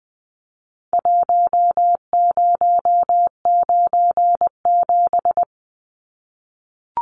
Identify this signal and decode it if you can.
Morse code, heard as a keyed tone, about 20 wpm, 700 Hz. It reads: 1097
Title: Numbers - Contest Style - 4-Digit 20wpm